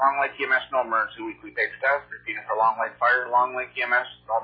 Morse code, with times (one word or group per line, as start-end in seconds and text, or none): none